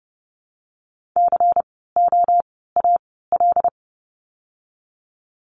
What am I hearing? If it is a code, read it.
Morse code, heard as a keyed tone, about 30 wpm, spaced as for 17 wpm, 700 Hz. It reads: COAL
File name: Lesson 16f - Letter c - Words 30wpm